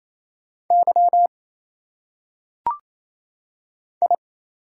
Morse code, heard as a keyed tone, about 28 wpm, 700 Hz. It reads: Y I
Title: Single Letters - Phonetic Pronunciation 28wpm